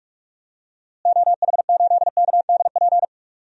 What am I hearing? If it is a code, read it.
Morse code, heard as a keyed tone, about 46 wpm, 700 Hz. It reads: OH8KDP
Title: DXCC-100 Call Signs - 6 Characters - Repeated 2x - 45wpm